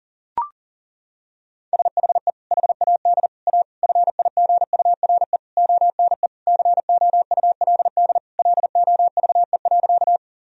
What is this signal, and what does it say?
Morse code, heard as a keyed tone, about 37 wpm, 700 Hz. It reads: SHE HAD A FIGURE ONE COULD LOVE.
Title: Sentences from Top 1000 Words - Review Part 2 - 40wpm